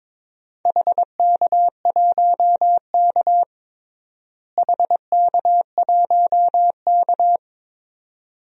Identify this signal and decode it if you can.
Morse code, heard as a keyed tone, about 21 wpm, 700 Hz. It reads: HK1K HK1K
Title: DXCC-100 Call Signs - 4 Characters - Repeated 2x - 22wpm